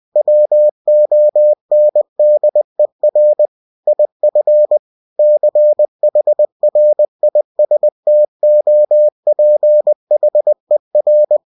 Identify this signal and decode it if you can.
Morse code, heard as a keyed tone, about 20 wpm, 600 Hz. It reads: WONDER IF CHRISTOPHER
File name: cw-books / winnie-the-pooh / winnie-20wpm-page-0005